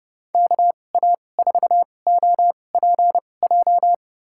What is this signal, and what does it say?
Morse code, heard as a keyed tone, about 30 wpm, spaced as for 22 wpm, 700 Hz. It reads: KA4OPJ
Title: USA Call Signs - 6 Characters - 2x Character Spacing - 30wpm